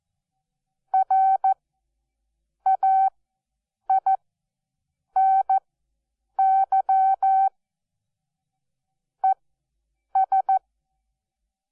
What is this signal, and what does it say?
Morse code, heard as a keyed tone, about 14 wpm, spaced as for 7 wpm, 770 Hz. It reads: RAINY ES
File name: Onlinelehrgang / morsen / DJ4UF Morsen12